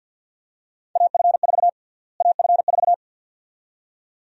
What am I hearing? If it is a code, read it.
Morse code, heard as a keyed tone, about 52 wpm, 700 Hz. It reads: AF4 AF4
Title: USA Call Sign Prefixes - Repeated 2x - 50wpm